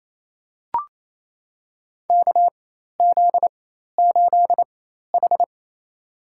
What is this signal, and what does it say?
Morse code, heard as a keyed tone, about 28 wpm, spaced as for 13 wpm, 700 Hz. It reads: KZ8H
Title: USA Call Signs - 4 Characters - 4x Character Spacing - 28wpm